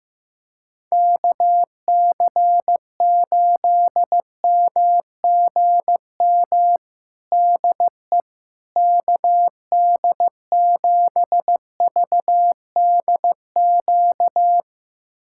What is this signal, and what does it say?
Morse code, heard as a keyed tone, about 15 wpm, 700 Hz. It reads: KC8MGM DE KD7VDQ
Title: USA Callsign DE Callsign - 6 Characters - 15wpm